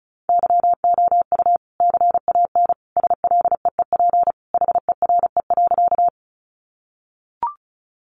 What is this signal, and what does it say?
Morse code, heard as a keyed tone, about 34 wpm, 700 Hz. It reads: YOU CAN SLEEP HERE.
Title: Sentences from Top 500 Words 35wpm